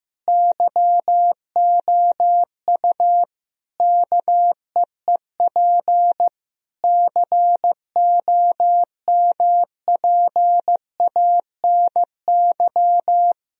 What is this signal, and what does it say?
Morse code, heard as a keyed tone, about 15 wpm, 700 Hz. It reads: YOU KEEP COMPANY
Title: Sentences from Top 1000 Words - Review Part 3 - 15wpm